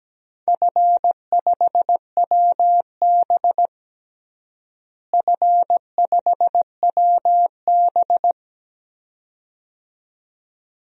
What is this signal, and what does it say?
Morse code, heard as a keyed tone, about 17 wpm, 700 Hz. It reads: F5WB F5WB
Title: DXCC-100 Call Signs - 4 Characters - Repeated 2x - 17wpm